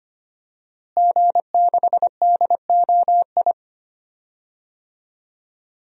Morse code, heard as a keyed tone, about 25 wpm, 700 Hz. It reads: G6DOI
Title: DXCC-100 Call Signs - 5 Characters - Repeated 3x - 25wpm